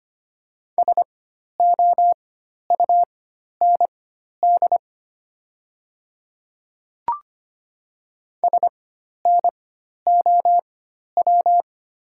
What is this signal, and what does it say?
Morse code, heard as a keyed tone, about 25 wpm, spaced as for 12 wpm, 690 Hz. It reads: SOUND SNOW
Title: 4x Character Spacing - Top 500 Words - 25wpm